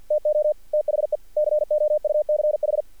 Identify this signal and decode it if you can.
Morse code, heard as a keyed tone, about 46 wpm, 600 Hz. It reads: TO THE COUCH
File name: archived / CRSnet